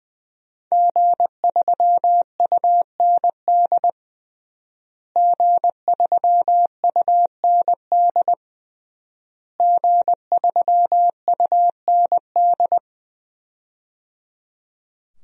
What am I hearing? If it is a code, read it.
Morse code, heard as a keyed tone, about 20 wpm, 700 Hz. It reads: G3UND G3UND G3UND
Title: DXCC-100 Call Signs - 5 Characters - Repeated 3x - 20wpm